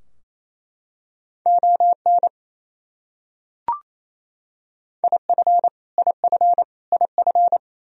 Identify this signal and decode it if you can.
Morse code, heard as a keyed tone, about 28 wpm, 700 Hz. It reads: ON IF IF IF
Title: Top 100 Words Repeated 28wpm